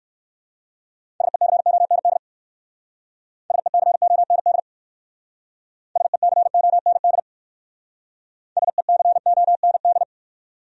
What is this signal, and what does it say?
Morse code, heard as a keyed tone, about 62 wpm, 700 Hz. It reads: SECOND SECOND SECOND SECOND